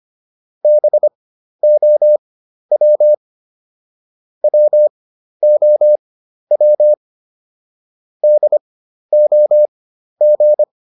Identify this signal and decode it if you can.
Morse code, heard as a keyed tone, about 25 wpm, spaced as for 12 wpm, 600 Hz. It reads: BOW WOW DOG